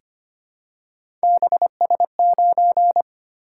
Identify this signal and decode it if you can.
Morse code, heard as a keyed tone, about 25 wpm, 700 Hz. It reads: BS9
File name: Three Letter-Number Combinations - 25wpm